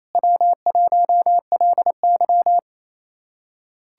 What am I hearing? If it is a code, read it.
Morse code, heard as a keyed tone, about 28 wpm, 700 Hz. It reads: W1LY
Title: USA Callsign DE Callsign - 4 Characters - 28wpm